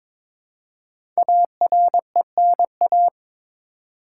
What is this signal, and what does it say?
Morse code, heard as a keyed tone, about 22 wpm, 700 Hz. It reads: ARENA
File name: 5 Letter Words - First 3 Letters Given - 22wpm